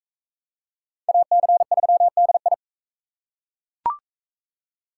Morse code, heard as a keyed tone, about 39 wpm, 700 Hz. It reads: AC3DI